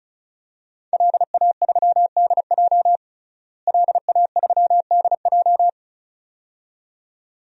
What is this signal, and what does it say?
Morse code, heard as a keyed tone, about 36 wpm, 700 Hz. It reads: LA3DJ LA3DJ
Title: DXCC-100 Call Signs - 5 Characters - Repeated 2x - 35wpm